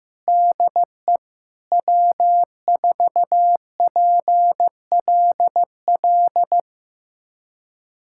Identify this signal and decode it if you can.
Morse code, heard as a keyed tone, about 15 wpm, 700 Hz. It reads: DE W4PLL
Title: USA Callsign DE Callsign - 5 Characters - 15wpm